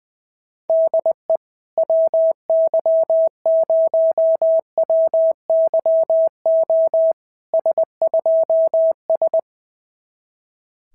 Morse code, heard as a keyed tone, about 20 wpm, 650 Hz. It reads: DE WY0WYO S2S
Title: SOTA QSO Practice 20wpm